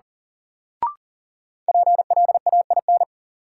Morse code, heard as a keyed tone, about 37 wpm, 700 Hz. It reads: PLAIN
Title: Sentences from Top 400 Words - New Word Drill - 40wpm